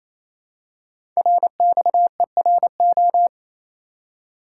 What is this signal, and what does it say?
Morse code, heard as a keyed tone, about 28 wpm, 700 Hz. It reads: RXERO
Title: Code Groups - 5 Letters - 28wpm